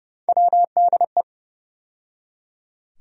Morse code, heard as a keyed tone, about 32 wpm, 700 Hz. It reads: WDE
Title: USA Call Sign Postfixes - 3 Letters 30wpm